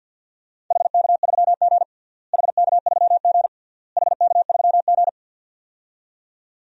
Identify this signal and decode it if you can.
Morse code, heard as a keyed tone, about 50 wpm, 700 Hz. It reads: HK3G HK3G HK3G
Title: DXCC-100 Call Signs - 4 Characters - Repeated 3x - 50wpm